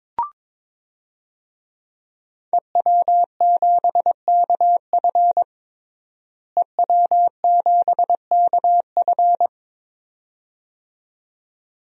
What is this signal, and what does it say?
Morse code, heard as a keyed tone, about 21 wpm, 700 Hz. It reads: EW7KF EW7KF
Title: DXCC-100 Call Signs - 5 Characters - Repeated 2x - 22wpm